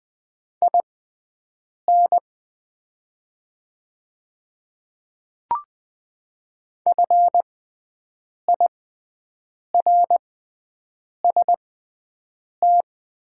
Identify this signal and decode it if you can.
Morse code, heard as a keyed tone, about 20 wpm, spaced as for 7 wpm, 700 Hz. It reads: IN FIRST
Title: 6x Character Spacing - Top 500 Words - 20wpm